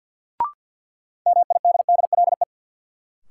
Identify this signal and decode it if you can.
Morse code, heard as a keyed tone, about 53 wpm, 700 Hz. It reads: MIDDLE
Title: Sets of 2 Words - New Word Drill - 50wpm